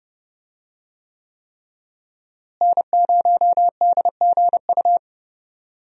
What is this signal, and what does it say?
Morse code, heard as a keyed tone, about 30 wpm, 700 Hz. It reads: N0DGU
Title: USA Call Signs - 5 Characters - Repeated 3x - 30wpm